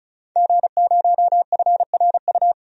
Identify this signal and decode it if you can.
Morse code, heard as a keyed tone, about 34 wpm, 700 Hz. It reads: G0FRU
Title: DXCC-100 Call Signs - 5 Characters - 35wpm